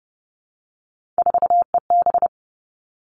Morse code, heard as a keyed tone, about 30 wpm, 700 Hz. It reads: 4EB
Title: All Cut Numbers - 3-Digit 30wpm